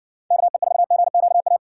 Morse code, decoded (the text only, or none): KE4GQA